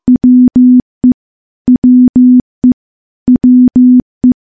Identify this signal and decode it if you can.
Morse code, heard as a keyed tone, about 15 wpm, 250 Hz. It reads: WE WE WE
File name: morse / morse-training-15WPM